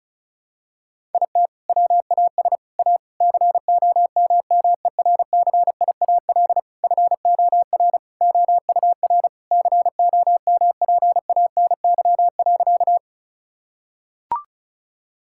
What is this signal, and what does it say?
Morse code, heard as a keyed tone, about 34 wpm, 700 Hz. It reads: IT WAS A COMMERCIAL FOR OUR COMPANY.
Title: Sentences from Top 1000 Words - Review Part 5 - 35wpm